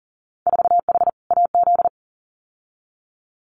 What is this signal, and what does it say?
Morse code, heard as a keyed tone, about 39 wpm, 700 Hz. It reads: 4H AZ
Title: Winter Field Day Exchanges Encore 40wpm